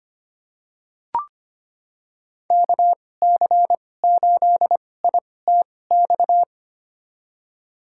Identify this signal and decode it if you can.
Morse code, heard as a keyed tone, about 25 wpm, spaced as for 18 wpm, 700 Hz. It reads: KC8ITX
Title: USA Call Signs - 2x Character Spacing - 25wpm